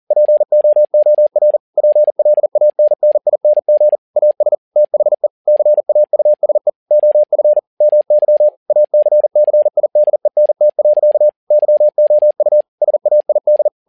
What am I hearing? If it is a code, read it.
Morse code, heard as a keyed tone, about 40 wpm, 600 Hz. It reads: POOR PLANNING AS THE CAUSE OF MY ACCIDENT. YOU SAID
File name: learning center / tower trouble / tower touble 40wpm